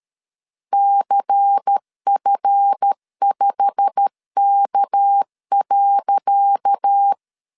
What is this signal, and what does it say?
Morse code, heard as a keyed tone, about 13 wpm, 800 Hz. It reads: CF5K.
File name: LEZIONI / 10 SIGNS / 10C DOT 12wpm 2451